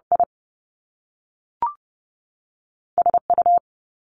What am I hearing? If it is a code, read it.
Morse code, heard as a keyed tone, about 30 wpm, 700 Hz. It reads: I S U
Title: Top Two Letter Combinations in Words 30wpm